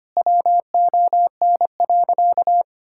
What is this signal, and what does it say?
Morse code, heard as a keyed tone, about 25 wpm, 700 Hz. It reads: WON.
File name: practice / 25wpm / Sentences with 3-Word Phrases - 25wpm